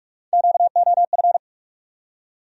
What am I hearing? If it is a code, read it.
Morse code, heard as a keyed tone, about 48 wpm, 700 Hz. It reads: QOF
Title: USA Call Sign Postfixes - 3 Letters 45wpm